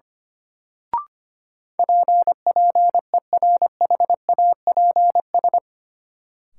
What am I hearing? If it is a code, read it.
Morse code, heard as a keyed tone, about 25 wpm, 700 Hz. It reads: P PERHAPS
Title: Sentences from Top 1100 Words - New Word Drill Alt - 25wpm